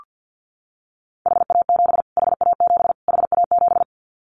Morse code, heard as a keyed tone, about 48 wpm, 700 Hz. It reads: HA7 HA7 HA7
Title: Callsign Prefixes for Top 25 DXCC - Repeated 3x - 50wpm